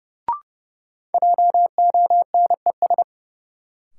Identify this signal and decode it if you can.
Morse code, heard as a keyed tone, about 30 wpm, 700 Hz. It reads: JONES
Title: Sets of 5 Words - New Word Drill - 30wpm